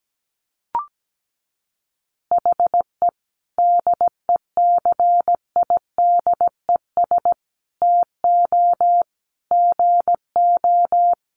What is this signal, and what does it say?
Morse code, heard as a keyed tone, about 17 wpm, 700 Hz. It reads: HE DECIDES TO GO
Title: Easy Phrases 17wpm